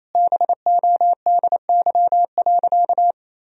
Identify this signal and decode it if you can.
Morse code, heard as a keyed tone, about 28 wpm, 700 Hz. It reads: BODY.
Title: Sentences from Top 1000 Words - Review Part 2 - 28wpm